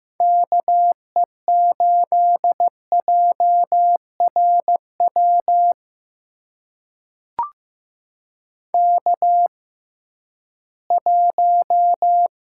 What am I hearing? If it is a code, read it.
Morse code, heard as a keyed tone, about 15 wpm, spaced as for 7 wpm, 700 Hz. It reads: KE8JR WK1
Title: USA Call Signs - 6x Character Spacing - 15wpm